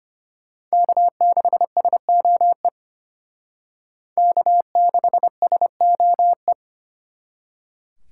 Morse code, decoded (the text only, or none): K6SOE K6SOE